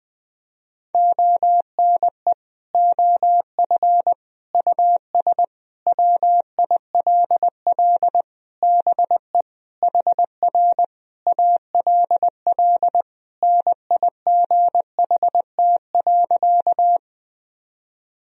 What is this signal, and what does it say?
Morse code, heard as a keyed tone, about 21 wpm, 700 Hz. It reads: ONE OF US WILL BE HR ALL NIGHT.
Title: Sentences from Top 300 Words and Abbreviations 20wpm